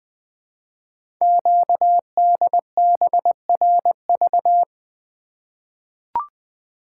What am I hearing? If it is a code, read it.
Morse code, heard as a keyed tone, about 20 wpm, 700 Hz. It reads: QDBRV
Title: Code Groups - 5 Letters - 20wpm